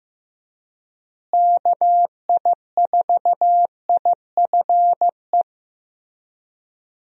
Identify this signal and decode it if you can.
Morse code, heard as a keyed tone, about 16 wpm, 700 Hz. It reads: KI4IFE
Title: USA Call Signs - 6 Characters - Repeated 3x - 15wpm